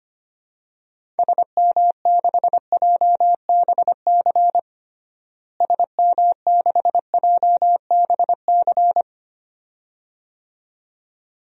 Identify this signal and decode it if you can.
Morse code, heard as a keyed tone, about 25 wpm, 700 Hz. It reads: SM6JBC SM6JBC
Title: DXCC-100 Call Signs - 6 Characters - Repeated 2x - 25wpm